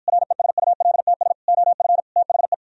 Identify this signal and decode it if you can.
Morse code, heard as a keyed tone, about 52 wpm, 700 Hz. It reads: RESULTS OF THE